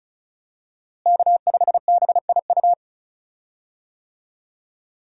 Morse code, heard as a keyed tone, about 36 wpm, 700 Hz. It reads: K5BIU
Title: USA Call Signs - 5 Characters - Repeated 2x - 35wpm